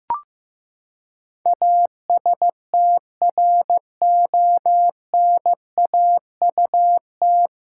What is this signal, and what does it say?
Morse code, heard as a keyed tone, about 15 wpm, 700 Hz. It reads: ASTRONAUT
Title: Pangram Challenge v2 - 15wpm